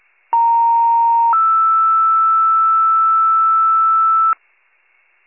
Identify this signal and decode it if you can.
Two-tone page, heard as a one-second tone, then a three-second tone, about 930 then 1320 Hz, with noise. It is heard from 0.3 to 4.3 s.